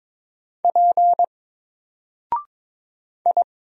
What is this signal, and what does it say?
Morse code, heard as a keyed tone, about 22 wpm, 700 Hz. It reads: P I
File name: Single-Letter-Number-22wpm